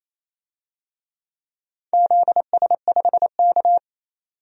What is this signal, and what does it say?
Morse code, heard as a keyed tone, about 28 wpm, 700 Hz. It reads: ZS5K
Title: DXCC-100 Call Signs - 4 Characters - 28wpm